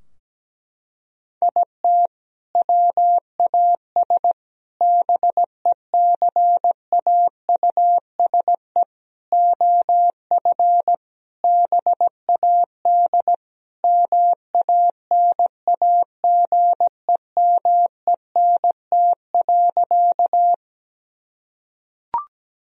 Morse code, heard as a keyed tone, about 17 wpm, 700 Hz. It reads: IT WAS BECAUSE OF BAD MANAGEMENT.
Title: Sentences from Top 1000 Words - Review Part 4 - 17wpm